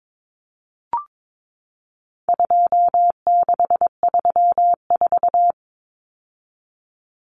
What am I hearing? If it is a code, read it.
Morse code, heard as a keyed tone, about 22 wpm, 700 Hz. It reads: 2634
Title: Numbers - 4-Digit 22wpm